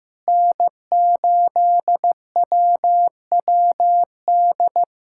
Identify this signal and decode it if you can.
Morse code, heard as a keyed tone, about 16 wpm, 700 Hz. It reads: N8WWD
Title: USA Callsign DE Callsign 15wpm